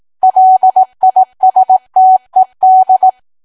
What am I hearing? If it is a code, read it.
Morse code, heard as a keyed tone, about 18 wpm, 750 Hz. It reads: LISTED